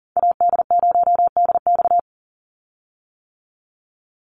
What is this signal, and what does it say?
Morse code, heard as a keyed tone, about 40 wpm, 700 Hz. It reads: AD0DX
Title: Call Signs - 40wpm